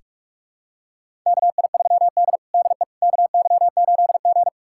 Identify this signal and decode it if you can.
Morse code, heard as a keyed tone, about 41 wpm, 700 Hz. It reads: KI3D DE KY8G
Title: USA Callsign DE Callsign - 4 Characters - 45wpm